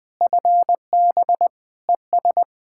Morse code, heard as a keyed tone, about 20 wpm, 700 Hz. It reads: FB ES
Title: Practice QSO - Exchange 2 - Info - 20wpm